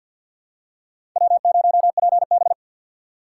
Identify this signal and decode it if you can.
Morse code, heard as a keyed tone, about 51 wpm, 700 Hz. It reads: W0PB
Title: USA Call Signs - 4 Characters - 3x Character Spacing - 50wpm